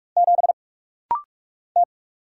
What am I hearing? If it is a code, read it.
Morse code, heard as a keyed tone, about 44 wpm, 700 Hz. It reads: 7 T
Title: Single Letter-Number - Rapid-Fire 45wpm